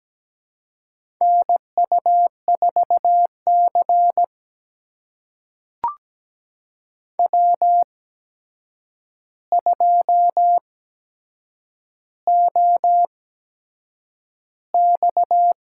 Morse code, heard as a keyed tone, about 17 wpm, spaced as for 7 wpm, 700 Hz. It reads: NU4C W2OX
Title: USA Call Signs - 4 Characters - 8x Character Spacing - 17wpm